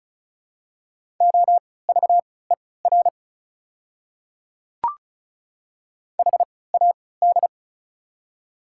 Morse code, heard as a keyed tone, about 34 wpm, spaced as for 20 wpm, 700 Hz. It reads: OVER HAD
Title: 3x Character Spacing - Top 500 Words - 35wpm